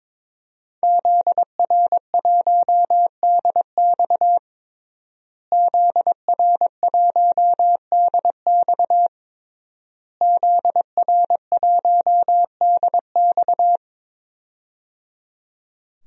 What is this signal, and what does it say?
Morse code, heard as a keyed tone, about 21 wpm, 700 Hz. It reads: ZR1DX ZR1DX ZR1DX
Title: DXCC-100 Call Signs - 5 Characters - Repeated 3x - 22wpm